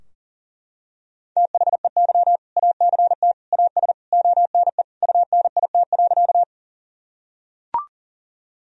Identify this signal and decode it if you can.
Morse code, heard as a keyed tone, about 40 wpm, 700 Hz. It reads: THEY ACT AS ONE UNIT.
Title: Sentences from Top 1000 Words - Review Part 2 - 40wpm